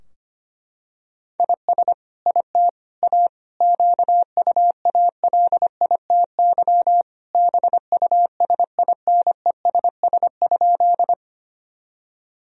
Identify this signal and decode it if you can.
Morse code, heard as a keyed tone, about 25 wpm, 700 Hz. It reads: IS IT A QUALITY BUSINESS?
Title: Sentences from Top 1000 Words - Review Part 4 - 25wpm